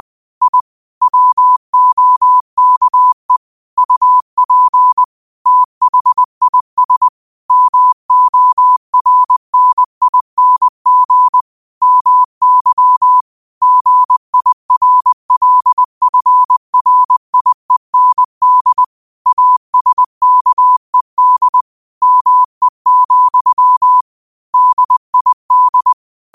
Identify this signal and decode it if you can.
Morse code, heard as a keyed tone, about 20 wpm, 1000 Hz. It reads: I WOKE UP THIS MORNING MY GIRLFRIEND ASKED ME, DID
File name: qotdfiles / 20wpm